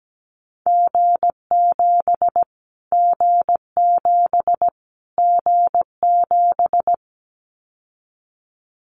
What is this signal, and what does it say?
Morse code, heard as a keyed tone, about 17 wpm, 700 Hz. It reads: G7 G7 G7
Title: Callsign Prefixes for Top 25 DXCC - Repeated 3x - 17wpm